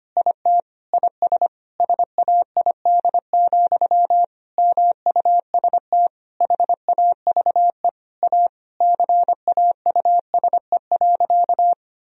Morse code, heard as a keyed tone, about 25 wpm, 700 Hz. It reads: IT IS SAID, MUST HAVE A CAUSE.
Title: Sentences from Top 1000 Words - Review Part 1 - 25wpm